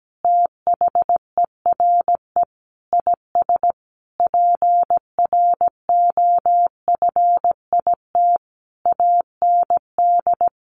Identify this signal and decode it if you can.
Morse code, heard as a keyed tone, about 17 wpm, 700 Hz. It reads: THERE IS PROFIT AND
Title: Sentences from Top 1400 Words 17wpm